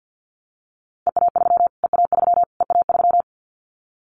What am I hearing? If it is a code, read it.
Morse code, heard as a keyed tone, about 50 wpm, 700 Hz. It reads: EA3 EA3 EA3
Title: Callsign Prefixes for Top 25 DXCC - Repeated 3x - 50wpm